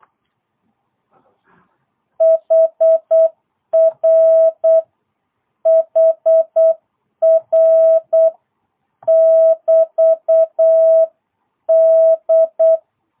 Morse code, heard as a keyed tone, about 8 wpm, 650 Hz. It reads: HR HR <BT> D